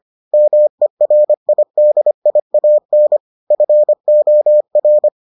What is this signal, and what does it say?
Morse code, heard as a keyed tone, about 25 wpm, 600 Hz. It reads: MERIDIAN FOR